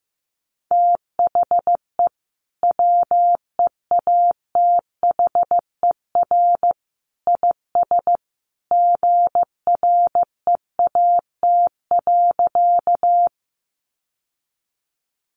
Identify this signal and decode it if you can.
Morse code, heard as a keyed tone, about 15 wpm, 700 Hz. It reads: THE WEATHER IS GREAT.
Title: Sentences from Top 800 Words 15wpm